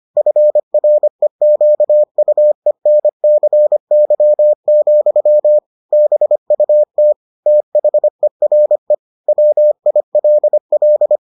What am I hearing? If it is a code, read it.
Morse code, heard as a keyed tone, about 25 wpm, 600 Hz. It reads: FREQUENCY, BUT THERE WILL